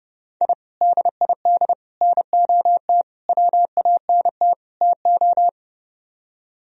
Morse code, heard as a keyed tone, about 29 wpm, 700 Hz. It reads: I DID NOT WANT TO